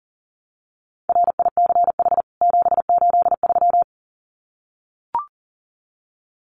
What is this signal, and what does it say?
Morse code, heard as a keyed tone, about 40 wpm, 700 Hz. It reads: RICH 783
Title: CWT Exchange 40wpm